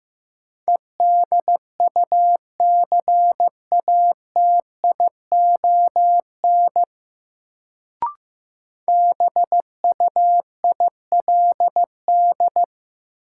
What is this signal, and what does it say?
Morse code, heard as a keyed tone, about 15 wpm, 700 Hz. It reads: EDUCATION BUILD
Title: Sentences from Top 1100 Words - New Word Drill Alt - 15wpm